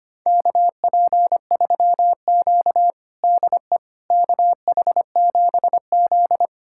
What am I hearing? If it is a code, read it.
Morse code, heard as a keyed tone, about 25 wpm, 700 Hz. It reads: KP3Q DE KH7Z
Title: USA Callsign DE Callsign - 4 Characters - 25wpm